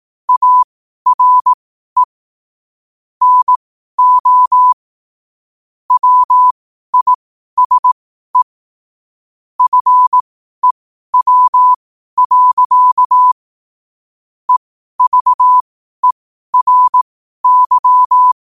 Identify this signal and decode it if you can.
Morse code, heard as a keyed tone, about 18 wpm, spaced as for 12 wpm, 1000 Hz. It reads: ARE NO WISE FEW. EVERY
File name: qotdfiles / 12wpm